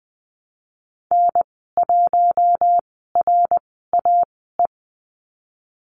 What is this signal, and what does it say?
Morse code, heard as a keyed tone, about 20 wpm, spaced as for 15 wpm, 700 Hz. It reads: N1RAE
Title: 069 Lesson 12g - Letter d - Call Signs 20wpm